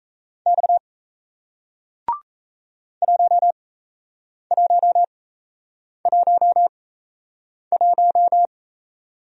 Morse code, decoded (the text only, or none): X 1111